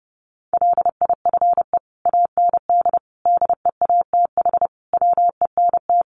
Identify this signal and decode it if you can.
Morse code, heard as a keyed tone, about 30 wpm, 700 Hz. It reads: LIFE AND DEATH WENT
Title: Sentences from Top 1100 Words 30wpm